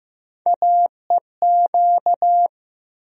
Morse code, heard as a keyed tone, about 15 wpm, 700 Hz. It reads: AEQ
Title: USA Call Sign Postfixes - 3 Letters 15wpm